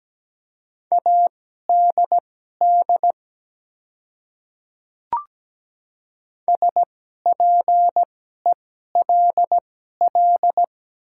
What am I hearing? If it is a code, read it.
Morse code, heard as a keyed tone, about 17 wpm, spaced as for 12 wpm, 700 Hz. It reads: ADD SPELL
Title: 2x Character Spacing - Top 200 Words - 17wpm